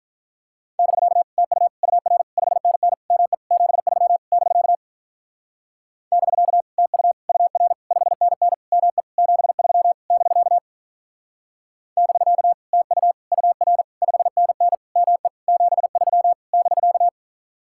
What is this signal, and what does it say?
Morse code, heard as a keyed tone, about 47 wpm, 700 Hz. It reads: <BK> TU UR 5NN ME 73 <BK> <BK> TU UR 5NN ME 73 <BK> <BK> TU UR 5NN ME 73 <BK>